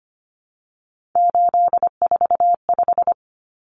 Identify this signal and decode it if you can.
Morse code, heard as a keyed tone, about 25 wpm, 700 Hz. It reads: 845
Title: Common Cut Numbers - 3-Digit 25wpm